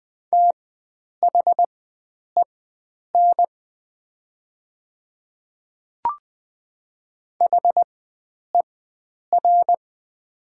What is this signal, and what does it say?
Morse code, heard as a keyed tone, about 20 wpm, spaced as for 9 wpm, 700 Hz. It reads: THEN HER
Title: 4x Character Spacing - Top 100 Words - 20wpm